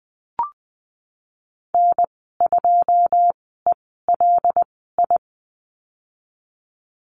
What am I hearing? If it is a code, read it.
Morse code, heard as a keyed tone, about 20 wpm, spaced as for 15 wpm, 700 Hz. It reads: N2ELI
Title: 088 Lesson 15g - Number 5 - Call Signs 20wpm